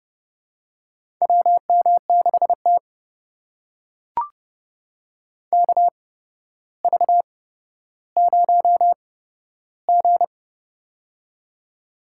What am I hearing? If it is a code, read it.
Morse code, heard as a keyed tone, about 30 wpm, spaced as for 13 wpm, 700 Hz. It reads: WM6T KV0G
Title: USA Call Signs - 4 Characters - 8x Character Spacing - 30wpm